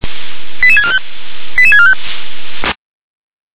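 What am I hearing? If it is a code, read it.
Five-tone selective call (ZVEI), heard as quick runs of five tones, tones of about 70 ms. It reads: 88645 88645
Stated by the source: ZVEI five-tone selective call received on an SDR.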